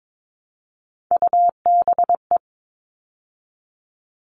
Morse code, heard as a keyed tone, about 22 wpm, 700 Hz. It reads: UBE
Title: All Cut Numbers - 3-Digit 22wpm